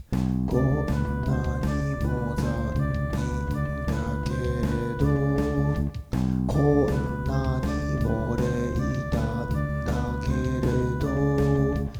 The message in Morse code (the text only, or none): none